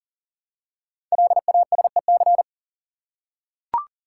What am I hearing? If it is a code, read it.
Morse code, heard as a keyed tone, about 42 wpm, 700 Hz. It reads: LASEC
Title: Code Groups - 5 Letters - 40wpm